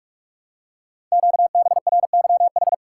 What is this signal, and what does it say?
Morse code, heard as a keyed tone, about 44 wpm, 700 Hz. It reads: QBRYH
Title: Code Groups - 5 Letters - 45wpm